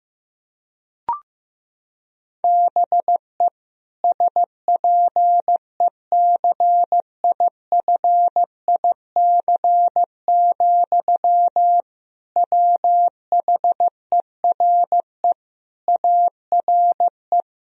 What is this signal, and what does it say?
Morse code, heard as a keyed tone, about 15 wpm, 700 Hz. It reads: BE SPECIFIC, WHERE ARE
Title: Sentences from Top 1000 Words - Review Part 3 - 15wpm